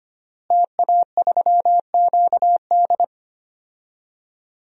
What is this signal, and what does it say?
Morse code, heard as a keyed tone, about 25 wpm, 700 Hz. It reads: TA3QD
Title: DXCC-100 Call Signs - 5 Characters - Repeated 3x - 25wpm